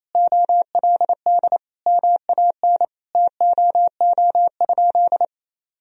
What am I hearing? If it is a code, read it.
Morse code, heard as a keyed tone, about 28 wpm, 700 Hz. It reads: OLD MAN TOO?
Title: Sentences from Top 1000 Words - Review Part 4 - 28wpm